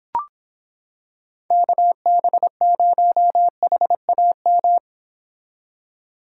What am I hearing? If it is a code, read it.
Morse code, heard as a keyed tone, about 26 wpm, 700 Hz. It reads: KB0HAM